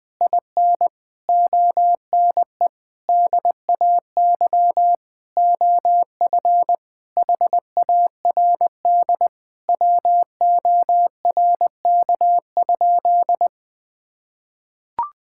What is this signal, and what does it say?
Morse code, heard as a keyed tone, about 20 wpm, 700 Hz. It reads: IN ONE DAY OF HARD WORK?
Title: Sentences from Top 1000 Words - Review Part 6 - 20wpm